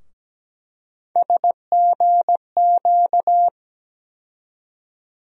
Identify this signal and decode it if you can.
Morse code, heard as a keyed tone, about 17 wpm, 700 Hz. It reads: SGQ
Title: USA Call Sign Postfixes - 3 Letters 17wpm